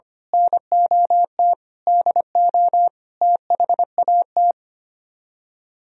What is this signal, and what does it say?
Morse code, heard as a keyed tone, about 24 wpm, 700 Hz. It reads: NOT DO THAT
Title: Speed-Racing - Sets of 5 Words 25wpm